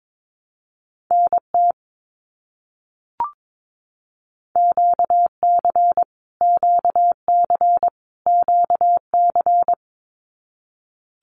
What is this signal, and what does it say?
Morse code, heard as a keyed tone, about 22 wpm, 700 Hz. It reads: NT QC QC QC
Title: Canadian Provinces and Territories - Repeated 3x - 22wpm